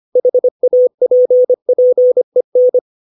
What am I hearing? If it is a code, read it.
Morse code, heard as a keyed tone, about 25 wpm, 500 Hz. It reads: HAPPEN
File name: 632 words / 632 Words 25WPM